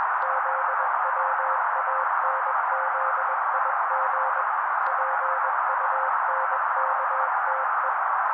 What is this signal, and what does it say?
Morse code, heard as a keyed tone, about 21 wpm, 530 Hz. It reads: ZWANZIG PUNKTE